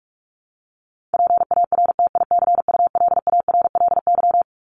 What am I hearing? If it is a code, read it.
Morse code, heard as a keyed tone, about 44 wpm, 700 Hz. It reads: PARTICULARLY
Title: LY Suffix - Top 100 Words with Base Words 45wpm